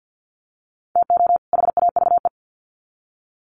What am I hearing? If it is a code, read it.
Morse code, heard as a keyed tone, about 50 wpm, 700 Hz. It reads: TO HAVE
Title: Sets of 2 Words 50wpm